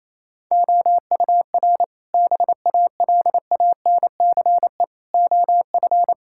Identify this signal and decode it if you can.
Morse code, heard as a keyed tone, about 28 wpm, 700 Hz. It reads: OUR BALANCE OF
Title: Sentences from Top 1000 Words - Review Part 6 - 28wpm